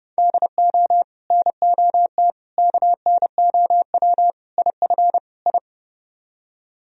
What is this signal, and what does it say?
Morse code, heard as a keyed tone, about 37 wpm, 700 Hz. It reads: DO NOT KNOW IF I
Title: Speed-Racing - Sets of 5 Words 30wpm